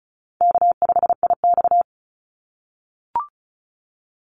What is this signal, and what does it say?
Morse code, heard as a keyed tone, about 34 wpm, 700 Hz. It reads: K5IX
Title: Call Signs - 35wpm